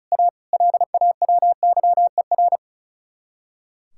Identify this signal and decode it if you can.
Morse code, heard as a keyed tone, about 42 wpm, 700 Hz. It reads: A LAWYER
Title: Speed-Racing Common QSO Phrases Encore - 35wpm